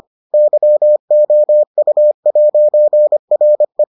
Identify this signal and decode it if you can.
Morse code, heard as a keyed tone, about 25 wpm, 600 Hz. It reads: YOU'RE